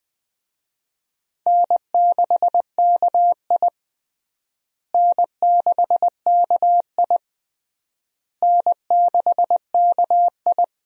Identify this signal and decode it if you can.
Morse code, heard as a keyed tone, about 20 wpm, 700 Hz. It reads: N6KI N6KI N6KI
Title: USA Call Signs - 4 Characters - Repeated 3x - 20wpm